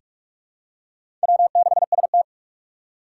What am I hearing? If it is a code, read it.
Morse code, heard as a keyed tone, about 47 wpm, 700 Hz. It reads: W6ST
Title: USA Call Signs - 4 Characters - 6x Character Spacing - 45wpm